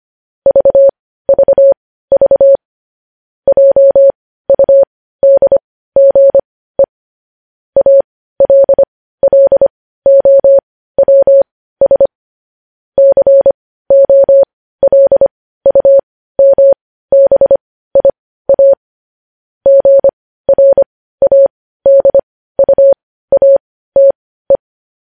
Morse code, heard as a keyed tone, about 25 wpm, spaced as for 15 wpm, 550 Hz. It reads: VVV JUDGE ALLOWS COLUMBIA GRADUATE
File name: podcast / News headlines-1x-25x15